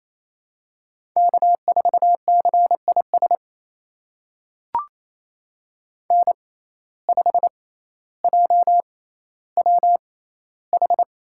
Morse code, heard as a keyed tone, about 28 wpm, spaced as for 14 wpm, 700 Hz. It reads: K4CIS N5JWH
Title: USA Call Signs - 5 Characters - 6x Character Spacing - 28wpm